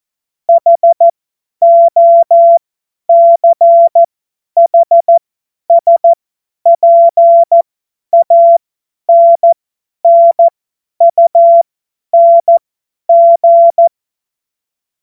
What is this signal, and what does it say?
Morse code, heard as a keyed tone, about 14 wpm, spaced as for 10 wpm, 680 Hz. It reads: HOCHSPANNUNG
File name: Morsekurs / Witze / tz 08. November 2022 50 F70